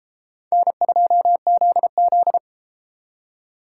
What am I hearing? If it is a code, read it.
Morse code, heard as a keyed tone, about 33 wpm, 700 Hz. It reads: N2ZZ